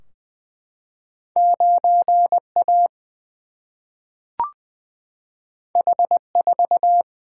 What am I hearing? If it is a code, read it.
Morse code, heard as a keyed tone, about 20 wpm, 700 Hz. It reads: 9 A H 4
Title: Single-Letter Single-Number 20wpm